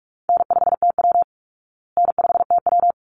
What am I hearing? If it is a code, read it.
Morse code, heard as a keyed tone, about 44 wpm, 700 Hz. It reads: N5TW N5TW
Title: Easy Call Signs Repeated 2x - 45wpm